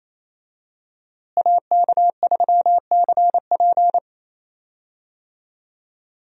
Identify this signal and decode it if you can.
Morse code, heard as a keyed tone, about 28 wpm, 700 Hz. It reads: AK3CP
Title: USA Call Signs - 5 Characters - Repeated 2x - 28wpm